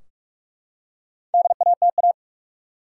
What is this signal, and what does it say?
Morse code, heard as a keyed tone, about 44 wpm, 700 Hz. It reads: DATA
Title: Four Letter Words - 45wpm